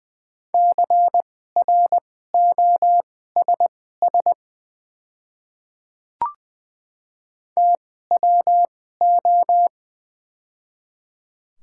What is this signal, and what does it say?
Morse code, heard as a keyed tone, about 20 wpm, spaced as for 15 wpm, 700 Hz. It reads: CROSS TWO
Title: 2x Character Spacing - Top 300 Words - 20wpm